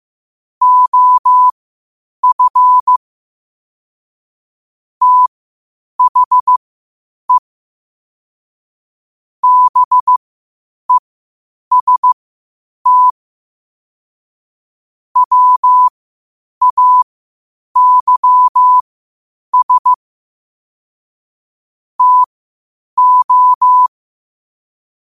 Morse code, heard as a keyed tone, about 15 wpm, spaced as for 8 wpm, 1000 Hz. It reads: OF THE BEST WAYS TO
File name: qotdfiles / 8wpm